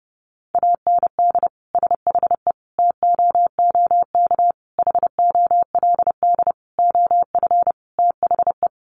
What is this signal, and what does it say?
Morse code, heard as a keyed tone, about 30 wpm, 700 Hz. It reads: AND SHE TOOK HOLD OF THE
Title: Sentences from Top 1200 Words 30wpm